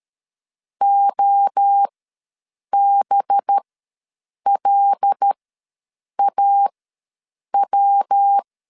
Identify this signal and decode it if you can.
Morse code, heard as a keyed tone, about 13 wpm, spaced as for 7 wpm, 800 Hz. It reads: OBLAW
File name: LEZIONI / 7 GF9Z / 7 G ws 12wpm 3140